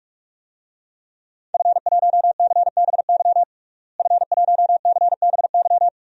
Morse code, heard as a keyed tone, about 46 wpm, 700 Hz. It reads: F1CBY F1CBY
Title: DXCC-100 Call Signs - 5 Characters - Repeated 3x - 45wpm